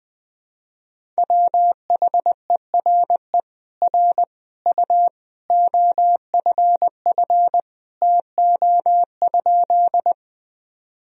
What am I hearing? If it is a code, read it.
Morse code, heard as a keyed tone, about 20 wpm, 700 Hz. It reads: WHERE R U OFF TO?
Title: Sentences from Top 200 Words and Abbreviations 20wpm